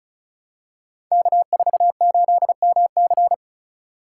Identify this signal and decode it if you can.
Morse code, heard as a keyed tone, about 35 wpm, 700 Hz. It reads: K48MC
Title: Code Groups - 5 Letter:Numbers - 35wpm